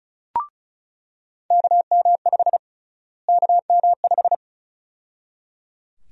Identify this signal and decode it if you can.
Morse code, heard as a keyed tone, about 34 wpm, 700 Hz. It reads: KM5 KM5
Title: USA Call Sign Prefixes - Repeated 2x - 35wpm